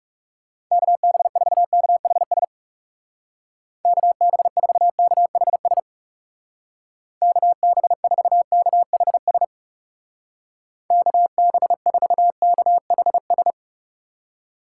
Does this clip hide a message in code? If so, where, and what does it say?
Yes: Morse code from 0.7 to 13.5 s, heard as a keyed tone, about 39 wpm, 700 Hz. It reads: KB4KHS KB4KHS KB4KHS KB4KHS